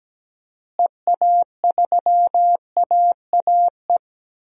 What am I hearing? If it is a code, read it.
Morse code, heard as a keyed tone, about 17 wpm, 700 Hz. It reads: EA3AAE
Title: DXCC-100 Call Signs - 6 Characters - Repeated 3x - 17wpm